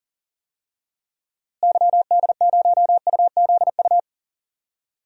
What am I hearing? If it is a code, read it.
Morse code, heard as a keyed tone, about 40 wpm, 700 Hz. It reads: YD0UZU
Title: DXCC-100 Call Signs - 6 Characters - Repeated 3x - 40wpm